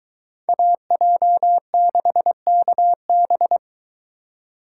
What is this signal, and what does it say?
Morse code, heard as a keyed tone, about 23 wpm, 700 Hz. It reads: AJ6KB